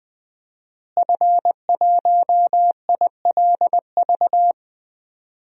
Morse code, heard as a keyed tone, about 20 wpm, 700 Hz. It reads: F1ILV
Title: DXCC-100 Call Signs - 5 Characters - Repeated 3x - 20wpm